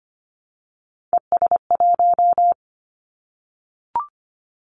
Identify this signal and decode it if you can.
Morse code, heard as a keyed tone, about 25 wpm, 690 Hz. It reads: ES1
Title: Callsign Prefixes for Top 50 DXCC - 25wpm